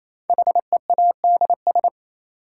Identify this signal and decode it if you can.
Morse code, heard as a keyed tone, about 28 wpm, 700 Hz. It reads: HEADS
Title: Sentences from Top 1000 Words - Review Part 3 - New Word Drill - 28wpm